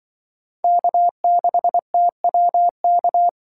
Morse code, heard as a keyed tone, about 24 wpm, 700 Hz. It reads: K6TWK
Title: Speed Racing - USA 1x3 Call Signs 20wpm